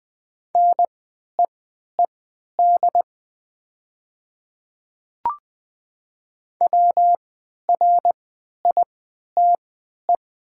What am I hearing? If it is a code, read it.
Morse code, heard as a keyed tone, about 20 wpm, spaced as for 11 wpm, 690 Hz. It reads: NEED WRITE
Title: 3x Character Spacing - Top 200 Words - 20wpm